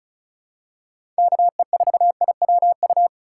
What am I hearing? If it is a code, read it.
Morse code, heard as a keyed tone, about 34 wpm, 700 Hz. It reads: KE4IWU
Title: USA Call Signs - 6x Character Spacing - 35wpm